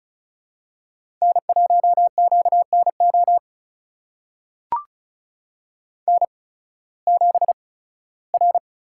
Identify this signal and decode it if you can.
Morse code, heard as a keyed tone, about 36 wpm, spaced as for 19 wpm, 700 Hz. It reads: N1QNO N7R
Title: USA Call Signs - 5 Characters - 8x Character Spacing - 35wpm